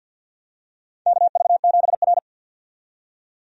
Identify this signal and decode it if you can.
Morse code, heard as a keyed tone, about 47 wpm, 700 Hz. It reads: KV7R
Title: USA Call Signs - 4 Characters - 50wpm